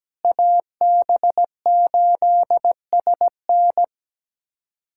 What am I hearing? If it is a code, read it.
Morse code, heard as a keyed tone, about 17 wpm, 700 Hz. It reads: AB8SN
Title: USA Callsign DE Callsign 17wpm